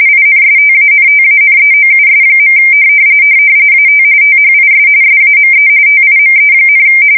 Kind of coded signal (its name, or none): RTTY